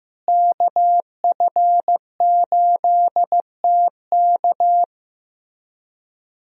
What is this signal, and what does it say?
Morse code, heard as a keyed tone, about 15 wpm, 700 Hz. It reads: KF8TK
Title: USA Call Signs - 5 Characters - Repeated 3x - 15wpm